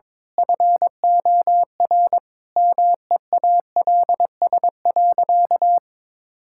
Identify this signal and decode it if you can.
Morse code, heard as a keyed tone, about 23 wpm, 700 Hz. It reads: FOR MEALS.
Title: Pangram Challenge v2 - 22wpm